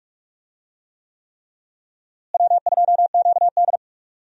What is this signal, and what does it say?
Morse code, heard as a keyed tone, about 46 wpm, 700 Hz. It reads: W2QD
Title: USA Call Signs - 4 Characters - Repeated 3x - 45wpm